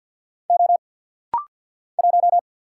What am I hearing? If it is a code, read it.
Morse code, heard as a keyed tone, about 50 wpm, 690 Hz. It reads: O 1
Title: Single Letter-Number Alt - Rapid-Fire 50wpm